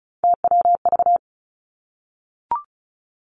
Morse code, heard as a keyed tone, about 36 wpm, 700 Hz. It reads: TWV
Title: Three Challenging Letter-Number Combinations 35wpm